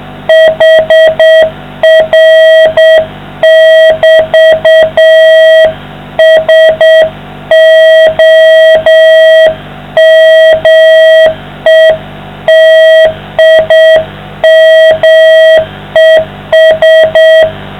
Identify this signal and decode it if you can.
Morse code, heard as a keyed tone, about 7 wpm, 640 Hz, with noise. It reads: HR<BT>SOMETIMES